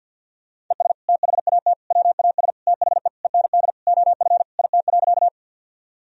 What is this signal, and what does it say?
Morse code, heard as a keyed tone, about 48 wpm, 700 Hz. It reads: ES THAT WAS THE END OF IT.
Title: Sentences from Top 200 Words and Abbreviations 50wpm